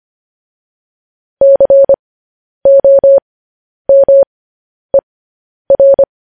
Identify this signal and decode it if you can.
Morse code, heard as a keyed tone, about 25 wpm, spaced as for 10 wpm, 550 Hz. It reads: COMER